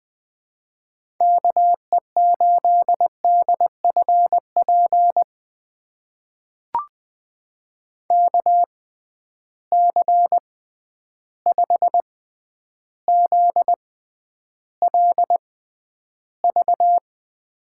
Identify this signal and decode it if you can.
Morse code, heard as a keyed tone, about 20 wpm, spaced as for 10 wpm, 700 Hz. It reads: KE8DFP KC5ZLV
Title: USA Call Signs - 6 Characters - 6x Character Spacing - 20wpm